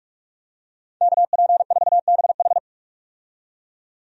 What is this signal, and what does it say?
Morse code, heard as a keyed tone, about 41 wpm, 700 Hz. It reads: KP4BH
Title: DXCC-100 Call Signs - 5 Characters - 45wpm